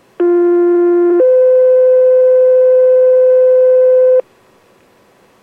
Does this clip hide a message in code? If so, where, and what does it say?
Yes: two-tone page from 0.2 to 4.2 s, heard as a one-second tone, then a three-second tone, about 350 then 510 Hz.